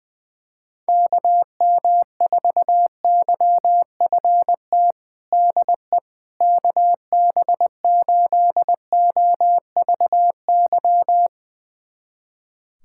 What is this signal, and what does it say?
Morse code, heard as a keyed tone, about 20 wpm, 700 Hz. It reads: KM4YFT DE KB8OVY
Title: USA Callsign DE Callsign - 6 Characters - 20wpm